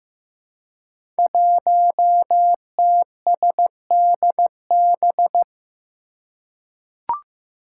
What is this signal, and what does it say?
Morse code, heard as a keyed tone, about 15 wpm, 700 Hz. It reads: 1TSDB
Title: Five Challenging Letter-Number Combinations 15wpm